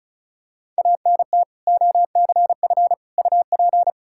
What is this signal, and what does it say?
Morse code, heard as a keyed tone, about 36 wpm, 700 Hz. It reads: ANT OCF UP
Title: Practice QSO - Exchange 2 - Info - 35wpm